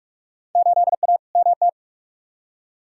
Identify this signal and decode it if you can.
Morse code, heard as a keyed tone, about 47 wpm, 700 Hz. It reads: 8A MT
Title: Field Day Exchanges Encore - 45wpm